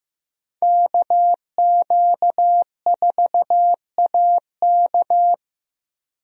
Morse code, heard as a keyed tone, about 15 wpm, 700 Hz. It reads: KQ4AK
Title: USA Call Signs - 5 Characters - Repeated 2x - 15wpm